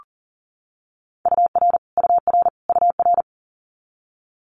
Morse code, heard as a keyed tone, about 40 wpm, 700 Hz. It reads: UR UR UR
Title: Abbreviations Repeated 3x - 40wpm